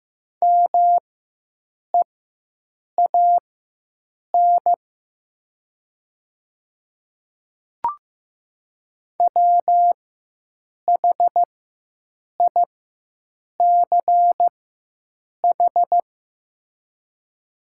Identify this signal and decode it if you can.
Morse code, heard as a keyed tone, about 15 wpm, spaced as for 7 wpm, 700 Hz. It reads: MEAN WHICH
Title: 4x Character Spacing - Top 500 Words - 15wpm